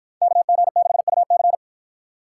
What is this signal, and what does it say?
Morse code, heard as a keyed tone, about 53 wpm, 700 Hz. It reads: KG6UC